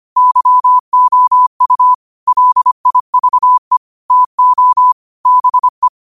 Morse code, heard as a keyed tone, about 25 wpm, 1000 Hz. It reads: YOU LIVE TO BE
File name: qotdfiles / 25wpm